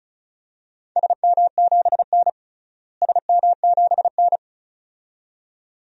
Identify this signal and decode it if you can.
Morse code, heard as a keyed tone, about 34 wpm, 700 Hz. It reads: SM7N SM7N
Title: DXCC-100 Call Signs - 4 Characters - Repeated 2x - 35wpm